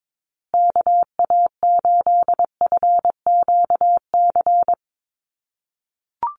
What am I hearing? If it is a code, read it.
Morse code, heard as a keyed tone, about 22 wpm, 700 Hz. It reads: KA8FQC
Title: USA Call Signs Repeated 3x 22wpm